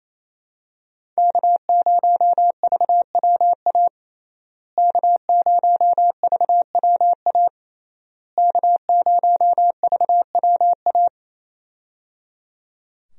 Morse code, heard as a keyed tone, about 28 wpm, 700 Hz. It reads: K0VWA K0VWA K0VWA
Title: USA Call Signs - 5 Characters - Repeated 3x - 28wpm